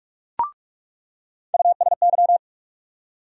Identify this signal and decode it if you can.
Morse code, heard as a keyed tone, about 43 wpm, 700 Hz. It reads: USY
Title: USA Call Sign Postfixes - 3 Letters 45wpm